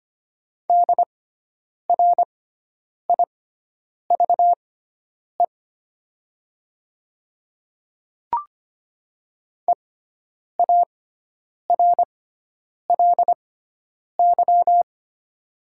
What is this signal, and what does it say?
Morse code, heard as a keyed tone, about 25 wpm, spaced as for 9 wpm, 700 Hz. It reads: DRIVE EARLY
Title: 6x Character Spacing - Top 500 Words - 25wpm